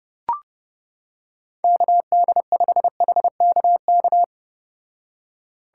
Morse code, heard as a keyed tone, about 30 wpm, 700 Hz. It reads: KD5HKK